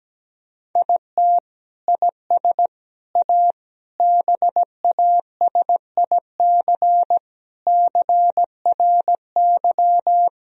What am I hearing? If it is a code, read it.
Morse code, heard as a keyed tone, about 17 wpm, 700 Hz. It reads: IT IS A BASIC CRY
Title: Sentences from Top 1000 Words - Review Part 3 - 17wpm